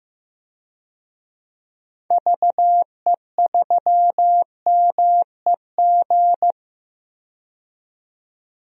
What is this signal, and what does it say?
Morse code, heard as a keyed tone, about 15 wpm, 700 Hz. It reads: VE3MEG
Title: DXCC-100 Call Signs - 6 Characters - 15wpm